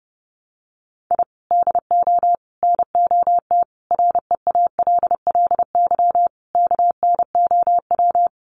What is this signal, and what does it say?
Morse code, heard as a keyed tone, about 30 wpm, 700 Hz. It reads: I DO NOT REALLY KNOW
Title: Sets of 5 Words Encore 30wpm